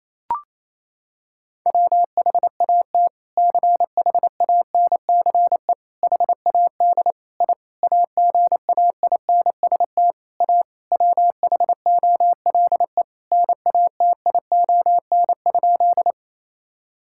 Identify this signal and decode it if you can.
Morse code, heard as a keyed tone, about 28 wpm, 700 Hz. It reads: WHAT CHANCE HAD I AGAINST A WHOLE NATION?
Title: Sentences from Top 1000 Words - Review Part 6 - 28wpm